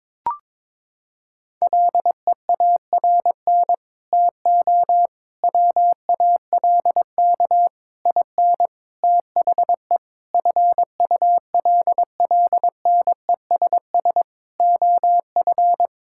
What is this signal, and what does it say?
Morse code, heard as a keyed tone, about 22 wpm, 700 Hz. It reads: LEARN TO WALK IN THE FULLNESS OF
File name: Sets of 9 Words - 22wpm